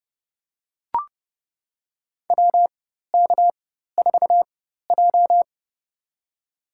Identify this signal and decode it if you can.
Morse code, heard as a keyed tone, about 31 wpm, spaced as for 14 wpm, 700 Hz. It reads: WK4J
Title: USA Call Signs - 4 Characters - 4x Character Spacing - 30wpm